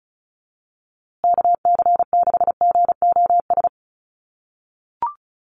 Morse code, heard as a keyed tone, about 34 wpm, 700 Hz. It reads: KC6GOS
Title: USA Call Signs Repeated 3x 35wpm